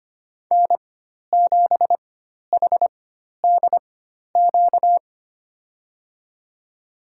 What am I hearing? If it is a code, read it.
Morse code, heard as a keyed tone, about 25 wpm, spaced as for 12 wpm, 700 Hz. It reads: N7HDQ
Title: USA Call Signs - 5 Characters - 4x Character Spacing - 25wpm